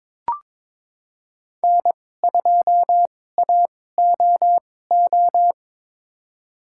Morse code, heard as a keyed tone, about 22 wpm, spaced as for 16 wpm, 700 Hz. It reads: N2AOO
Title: USA Call Signs - 5 Characters - 2x Character Spacing - 22wpm